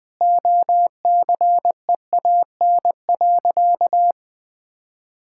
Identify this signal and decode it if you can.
Morse code, heard as a keyed tone, about 20 wpm, 700 Hz. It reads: OCEAN.
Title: Sentences with 8-Word Phrases - Spoken Beginning - 20wpm